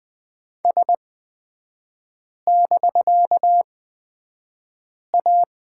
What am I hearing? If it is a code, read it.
Morse code, heard as a keyed tone, about 20 wpm, spaced as for 5 wpm, 700 Hz. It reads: S<BK>A
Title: 250 Lesson 40e - BK - Three Characters 20wpm